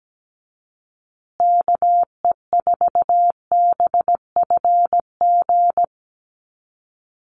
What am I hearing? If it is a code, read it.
Morse code, heard as a keyed tone, about 17 wpm, 700 Hz. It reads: KE4BFG
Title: USA Call Signs Repeated 3x 17wpm